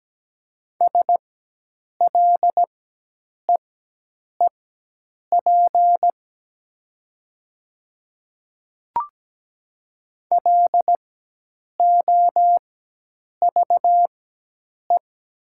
Morse code, heard as a keyed tone, about 17 wpm, spaced as for 8 wpm, 700 Hz. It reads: SLEEP LOVE
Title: 4x Character Spacing - Top 500 Words - 17wpm